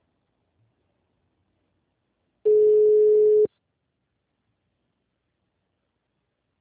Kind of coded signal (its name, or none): telephone call progress tone